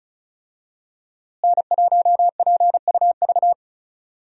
Morse code, heard as a keyed tone, about 36 wpm, 700 Hz. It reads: N1PUV
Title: USA Call Signs - 5 Characters - Repeated 2x - 35wpm